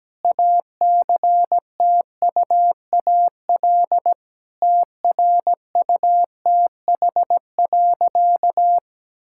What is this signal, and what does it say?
Morse code, heard as a keyed tone, about 17 wpm, 700 Hz. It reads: ACTUAL TRUTH.
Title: Sentences from Top 1000 Words - Review Part 5 - 17wpm